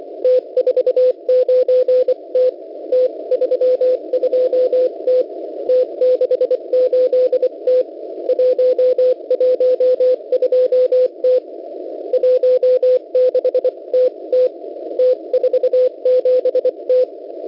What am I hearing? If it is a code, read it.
Morse code, heard as a keyed tone, about 24 wpm, 510 Hz. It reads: T49T T32T T68T 112T 16TT T47T
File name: Unid- -0700utc-4735khz